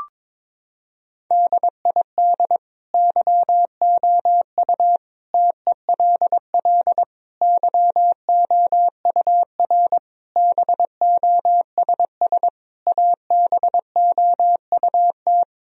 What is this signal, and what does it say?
Morse code, heard as a keyed tone, about 21 wpm, 700 Hz. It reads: DID YOU TELL YOUR BOSS ABOUT
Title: Sentences from Top 1000 Words - Review Part 4 - 22wpm